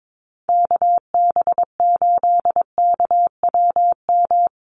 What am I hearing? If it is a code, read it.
Morse code, heard as a keyed tone, about 22 wpm, 700 Hz. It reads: KB8KWM
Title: USA Call Signs Repeated 3x 22wpm